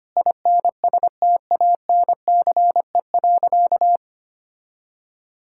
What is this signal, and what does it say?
Morse code, heard as a keyed tone, about 25 wpm, 700 Hz. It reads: INSTANCE.
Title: Sentences from Top 1000 Words - Review Part 5 - 25wpm